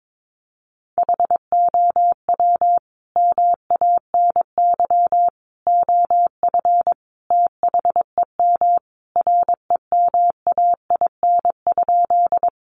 Morse code, heard as a keyed tone, about 22 wpm, 700 Hz. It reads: HOW MANY OF THEM REMAIN?
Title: Sentences from Top 1200 Words 22wpm